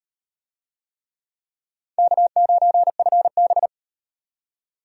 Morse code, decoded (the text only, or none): K9FB